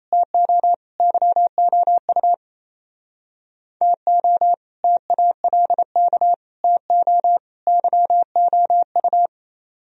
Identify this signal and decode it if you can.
Morse code, heard as a keyed tone, about 37 wpm, 690 Hz. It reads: TO YOU TO TALK TO YOU